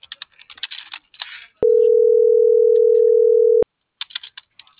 Ringback tone (North America), 440 + 480 Hz.
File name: CENTRAL AGUILERA / out brenda.ramos